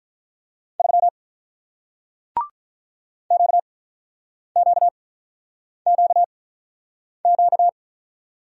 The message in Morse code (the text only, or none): 3 Q Q Q Q